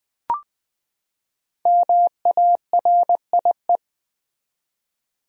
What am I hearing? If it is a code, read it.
Morse code, heard as a keyed tone, about 20 wpm, 700 Hz. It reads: MARIE
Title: USA Common Names - Max 5 Letters - 20wpm